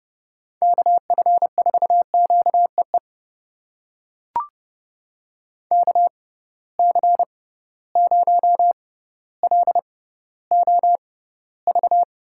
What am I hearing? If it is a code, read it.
Morse code, heard as a keyed tone, about 30 wpm, spaced as for 15 wpm, 700 Hz. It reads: KF4QEE KC0LOV
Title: USA Call Signs - 6 Characters - 6x Character Spacing - 30wpm